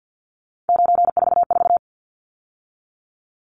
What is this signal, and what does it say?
Morse code, heard as a keyed tone, about 49 wpm, 700 Hz. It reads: 944
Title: Numbers - 3-Digit 50wpm